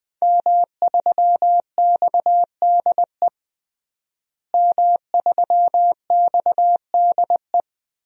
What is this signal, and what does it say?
Morse code, heard as a keyed tone, about 20 wpm, 700 Hz. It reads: M3XDE M3XDE
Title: DXCC-100 Call Signs - 5 Characters - Repeated 2x - 20wpm